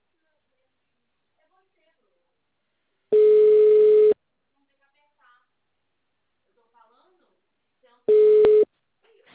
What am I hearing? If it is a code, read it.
Ringback tone (Europe, 425 Hz), with speech.